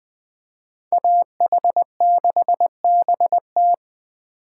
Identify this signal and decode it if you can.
Morse code, heard as a keyed tone, about 19 wpm, 700 Hz. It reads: AH6BT